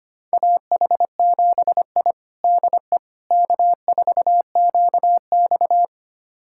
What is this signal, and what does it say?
Morse code, heard as a keyed tone, about 25 wpm, 700 Hz. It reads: AH7I DE K4QX
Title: USA Callsign DE Callsign - 4 Characters - 25wpm